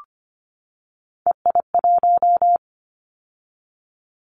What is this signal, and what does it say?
Morse code, heard as a keyed tone, about 25 wpm, 690 Hz. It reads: EI1
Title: Callsign Prefixes for Top 50 DXCC - 25wpm